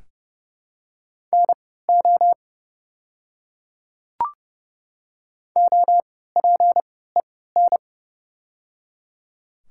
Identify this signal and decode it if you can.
Morse code, heard as a keyed tone, about 30 wpm, spaced as for 17 wpm, 700 Hz. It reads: NO OPEN
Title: 3x Character Spacing - Top 300 Words - 30wpm